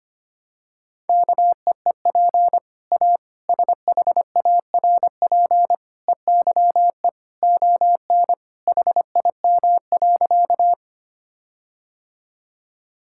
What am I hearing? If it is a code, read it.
Morse code, heard as a keyed tone, about 25 wpm, 700 Hz. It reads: KEEP A SHARP EYE ON HIM.
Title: Sentences from Top 1600 Words 25wpm